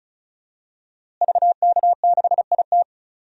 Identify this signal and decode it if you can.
Morse code, heard as a keyed tone, about 35 wpm, 700 Hz. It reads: VK6IT
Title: DXCC-100 Call Signs - 5 Characters - Repeated 3x - 35wpm